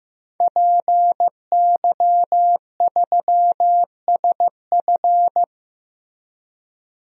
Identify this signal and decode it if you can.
Morse code, heard as a keyed tone, about 15 wpm, 700 Hz. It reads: PY3SF
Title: DXCC-100 Call Signs - 5 Characters - Repeated 2x - 15wpm